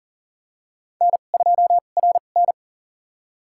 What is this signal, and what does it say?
Morse code, heard as a keyed tone, about 39 wpm, spaced as for 29 wpm, 690 Hz. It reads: N2RN
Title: USA Call Signs - 4 Characters - 2x Character Spacing - 40wpm